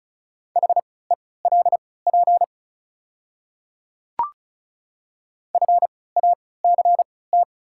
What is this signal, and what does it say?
Morse code, heard as a keyed tone, about 36 wpm, spaced as for 20 wpm, 700 Hz. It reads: HELP FACT
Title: 3x Character Spacing - Top 500 Words - 35wpm